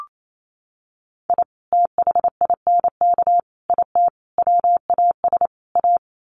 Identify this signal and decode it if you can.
Morse code, heard as a keyed tone, about 28 wpm, 700 Hz. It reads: I THINK IT WAS A
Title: Sets of 5 Words 28wpm